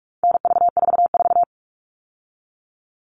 Morse code, heard as a keyed tone, about 44 wpm, 700 Hz. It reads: NV44
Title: All Cut Numbers - 4-Digit 45wpm